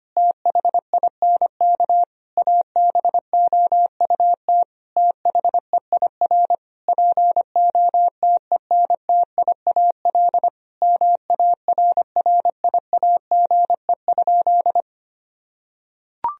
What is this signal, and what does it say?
Morse code, heard as a keyed tone, about 25 wpm, 700 Hz. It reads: THINK ABOUT THEIR POTENTIAL MARRIAGE?
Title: Sentences from Top 1000 Words - Review Part 4 - 25wpm